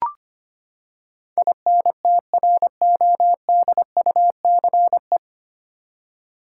Morse code, heard as a keyed tone, about 25 wpm, 700 Hz. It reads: INTRODUCE
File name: Nine Letter Words - 25wpm